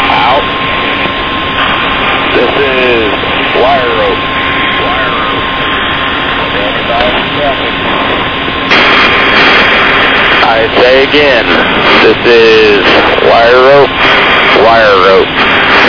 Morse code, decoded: none

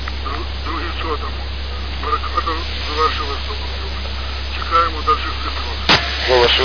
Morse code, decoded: none

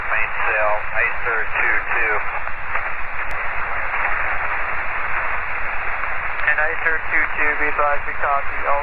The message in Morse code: none

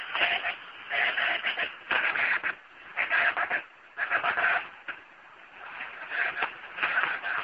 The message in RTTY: none

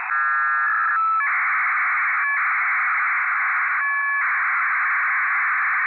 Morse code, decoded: none

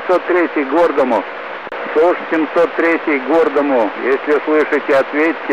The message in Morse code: none

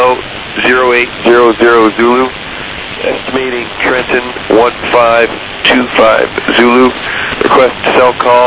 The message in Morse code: none

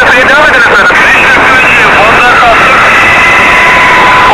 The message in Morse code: none

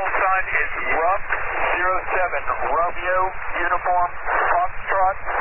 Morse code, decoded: none

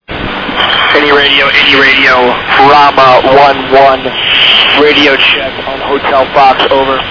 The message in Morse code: none

